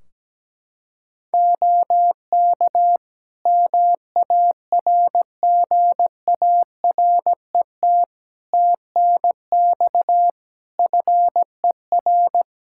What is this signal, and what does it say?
Morse code, heard as a keyed tone, about 17 wpm, 700 Hz. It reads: OK MARGARET TNX FER
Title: Practice QSO - Exchange 4 - Ending - 17wpm